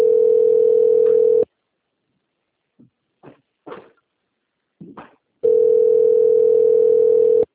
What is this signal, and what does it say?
Ringback tone (North America), 440 + 480 Hz.